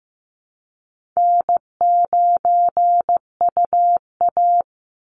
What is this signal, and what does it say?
Morse code, heard as a keyed tone, about 15 wpm, 700 Hz. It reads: N9UA
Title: Easy Call Signs Repeated 2x - 15wpm